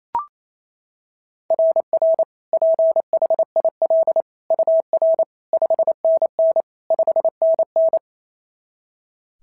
Morse code, heard as a keyed tone, about 28 wpm, 690 Hz. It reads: RR PHIL UR 5NN 5NN
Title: SOTA QSO Practice 28wpm